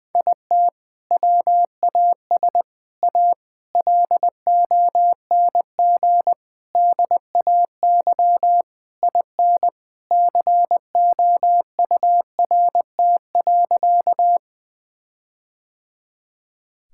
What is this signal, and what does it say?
Morse code, heard as a keyed tone, about 20 wpm, 700 Hz. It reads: IT WAS A LONG DAY IN COURT.
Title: Sentences from Top 1600 Words 20wpm